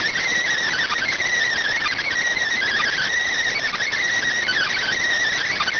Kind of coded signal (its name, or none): Emergency Alert System (SAME) tones